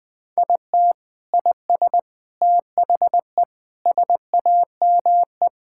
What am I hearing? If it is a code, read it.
Morse code, heard as a keyed tone, about 19 wpm, 700 Hz. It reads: IT IS THE SAME
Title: Sentences from Top 1600 Words 20wpm